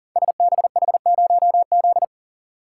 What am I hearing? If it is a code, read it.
Morse code, heard as a keyed tone, about 38 wpm, 690 Hz. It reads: SBH0Z
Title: Code Groups - 5 Letter:Numbers - 40wpm